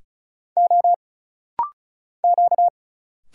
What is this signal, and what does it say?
Morse code, heard as a keyed tone, about 33 wpm, 700 Hz. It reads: O Q
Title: Single Letters - Rapid-Fire 35wpm